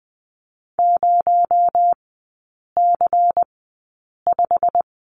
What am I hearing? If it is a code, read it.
Morse code, heard as a keyed tone, about 20 wpm, spaced as for 8 wpm, 700 Hz. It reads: 0C5
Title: 218 Lesson 35e - Number 0 - Three Characters 20wpm